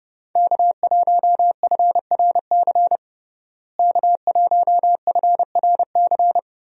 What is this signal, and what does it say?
Morse code, heard as a keyed tone, about 30 wpm, 700 Hz. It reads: K1FRC K1FRC
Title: USA Call Signs - 5 Characters - Repeated 2x - 30wpm